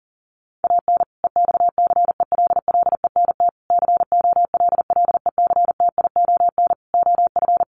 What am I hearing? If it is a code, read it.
Morse code, heard as a keyed tone, about 40 wpm, 700 Hz. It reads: AN EXCELLENT COLLECTION OF
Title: Sentences from Top 1200 Words 40wpm